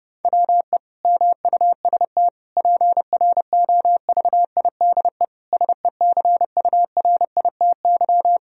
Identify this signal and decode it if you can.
Morse code, heard as a keyed tone, about 30 wpm, 700 Hz. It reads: WE MUST PROVIDE SECURITY
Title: Sentences from Top 1000 Words - Review Part 4 - 30wpm